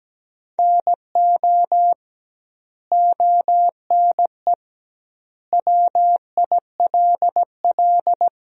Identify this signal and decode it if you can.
Morse code, heard as a keyed tone, about 17 wpm, 700 Hz. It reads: NO ONE WILL
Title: 2x Word-Spacing - Sentences from Top 100 Words - 17wpm